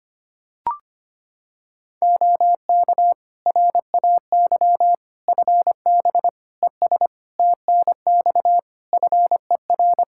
Morse code, heard as a keyed tone, about 25 wpm, 700 Hz. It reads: OK RAY FB ES TNX FER
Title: Practice QSO - Exchange 2 - Info - 25wpm